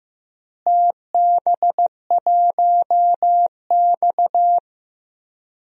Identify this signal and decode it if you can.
Morse code, heard as a keyed tone, about 15 wpm, 700 Hz. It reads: TB1X
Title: DXCC-100 Call Signs - 4 Characters - Repeated 2x - 15wpm